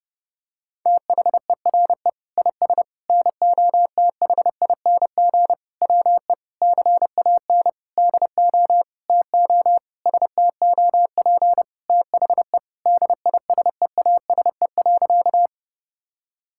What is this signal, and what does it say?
Morse code, heard as a keyed tone, about 28 wpm, 700 Hz. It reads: THERE IS NOTHING WE CAN DO TO STOP THE DISEASE.
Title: Sentences from Top 1000 Words - Review Part 6 - 30wpm